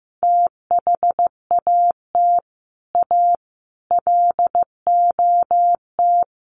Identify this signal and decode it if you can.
Morse code, heard as a keyed tone, about 15 wpm, 700 Hz. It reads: THAT A LOT
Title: Sets of 4 Words 15wpm